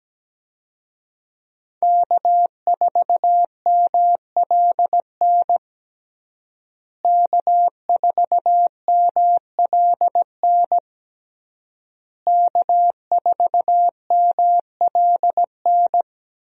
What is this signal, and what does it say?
Morse code, heard as a keyed tone, about 17 wpm, 700 Hz. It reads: K4MLN K4MLN K4MLN
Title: USA Call Signs - 5 Characters - Repeated 3x - 17wpm